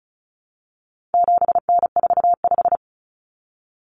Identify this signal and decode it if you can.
Morse code, heard as a keyed tone, about 34 wpm, 700 Hz. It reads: 7N45
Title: Common Cut Numbers - 4-Digit 35wpm